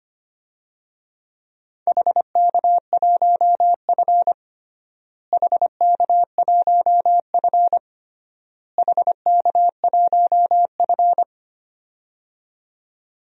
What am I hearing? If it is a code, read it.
Morse code, heard as a keyed tone, about 25 wpm, 700 Hz. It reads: HK1F HK1F HK1F
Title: DXCC-100 Call Signs - 4 Characters - Repeated 3x - 25wpm